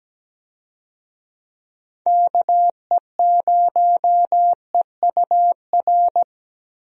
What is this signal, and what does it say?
Morse code, heard as a keyed tone, about 17 wpm, 700 Hz. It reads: KE0EUR
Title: USA Call Signs - 6 Characters - Repeated 2x - 17wpm